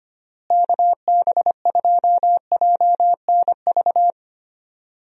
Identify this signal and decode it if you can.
Morse code, heard as a keyed tone, about 25 wpm, 700 Hz. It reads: KB2JNV
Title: USA Call Signs - 6 Characters - Repeated 3x - 25wpm